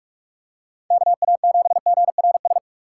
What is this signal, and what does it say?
Morse code, heard as a keyed tone, about 46 wpm, 700 Hz. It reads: KA7GRS
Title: USA Call Signs - 2x Character Spacing - 45wpm